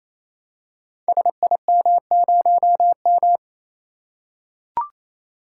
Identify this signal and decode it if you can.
Morse code, heard as a keyed tone, about 27 wpm, 700 Hz. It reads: SIM0M
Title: Code Groups - 5 Letter:Numbers - 28wpm